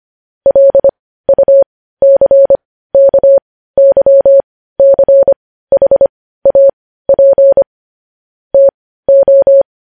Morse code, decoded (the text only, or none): LUCKYCHAP TO